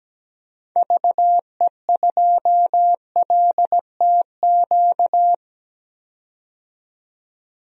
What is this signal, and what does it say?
Morse code, heard as a keyed tone, about 17 wpm, 700 Hz. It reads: VE2LTQ
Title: DXCC-100 Call Signs - 6 Characters - 17wpm